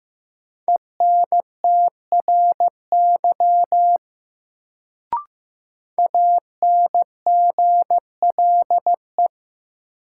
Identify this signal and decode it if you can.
Morse code, heard as a keyed tone, about 15 wpm, 700 Hz. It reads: ENTRY ANGLE
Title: Sentences from Top 1500 Words - New Word Drill - 15wpm